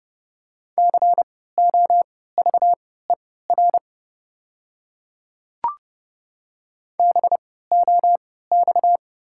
Morse code, heard as a keyed tone, about 30 wpm, spaced as for 17 wpm, 690 Hz. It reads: COVER BOX
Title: 3x Character Spacing - Top 500 Words - 30wpm